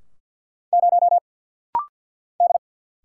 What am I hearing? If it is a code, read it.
Morse code, heard as a keyed tone, about 50 wpm, 700 Hz. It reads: 0 D
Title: Single Letter-Number - Rapid-Fire 50wpm